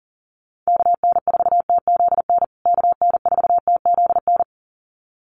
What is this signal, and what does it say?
Morse code, heard as a keyed tone, about 40 wpm, 700 Hz. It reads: KN4TZN KN4TZN
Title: USA Call Signs Repeated 2x 40wpm